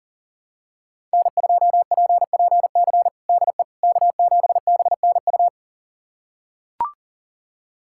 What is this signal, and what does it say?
Morse code, heard as a keyed tone, about 40 wpm, 690 Hz. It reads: N2PPC DE K7BNU
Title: USA Callsign DE Callsign - 5 Characters - 40wpm